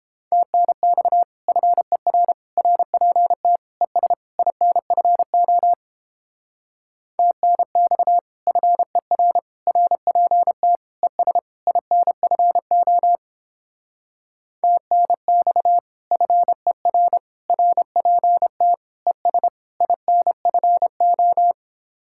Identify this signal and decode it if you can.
Morse code, heard as a keyed tone, about 29 wpm, 700 Hz. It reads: TNX FER RPT ES INFO TNX FER RPT ES INFO TNX FER RPT ES INFO